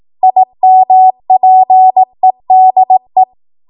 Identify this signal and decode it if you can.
Morse code, heard as a keyed tone, about 18 wpm, 750 Hz. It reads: IMPEDE
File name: Morse / 18WPM